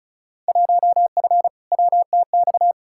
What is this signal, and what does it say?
Morse code, heard as a keyed tone, about 35 wpm, 700 Hz. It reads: 1F WTX
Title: Field Day Exchanges Encore - 35wpm